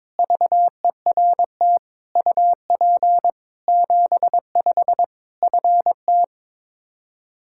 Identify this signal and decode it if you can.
Morse code, heard as a keyed tone, about 22 wpm, 700 Hz. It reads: VERT UP 75 FT
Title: Practice QSO - Exchange 2 - Info - 22wpm